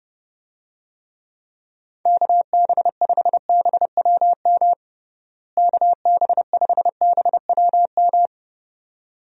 Morse code, heard as a keyed tone, about 32 wpm, 700 Hz. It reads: KB5BWM KB5BWM
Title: USA Call Signs - 6 Characters - Repeated 2x - 30wpm